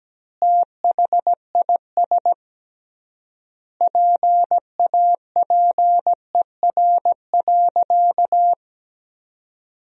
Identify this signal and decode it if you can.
Morse code, heard as a keyed tone, about 17 wpm, 700 Hz. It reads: THIS PAPER.
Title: 3x Word-Spacing - Sentences from Top 500 Words - 17wpm